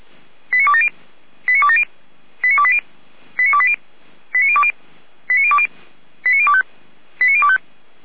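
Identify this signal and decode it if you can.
Five-tone selective call (ZVEI), heard as quick runs of five tones, tones of about 70 ms. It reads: 89270 89270 89280 89280 89020 89020 89025 89025
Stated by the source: Five-tone ZVEI selective call 89208, received on an SDR.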